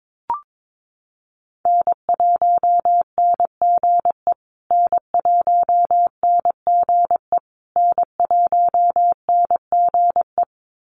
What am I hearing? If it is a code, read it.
Morse code, heard as a keyed tone, about 22 wpm, 700 Hz. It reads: N1NGE N1NGE N1NGE
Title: USA Call Signs Repeated 3x 22wpm